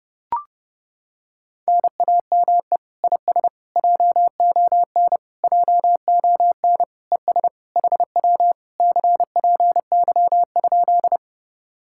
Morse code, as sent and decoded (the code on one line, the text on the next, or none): -. .- -- . / .. ... / .--- --- -. / .--- --- -. / . ... / .... .-- / -.-. .--. -.-- ..--..
NAME IS JON JON ES HW CPY?